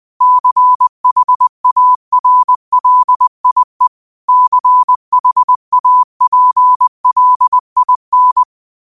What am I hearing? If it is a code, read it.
Morse code, heard as a keyed tone, about 20 wpm, 1000 Hz. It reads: CHARLIE CHAPLIN
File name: qotdfiles / 20wpm